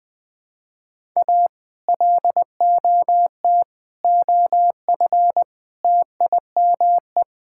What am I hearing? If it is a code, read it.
Morse code, heard as a keyed tone, about 21 wpm, 700 Hz. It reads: A LOT OF TIME
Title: Build A Phrase - 4-Word Phrases - 20wpm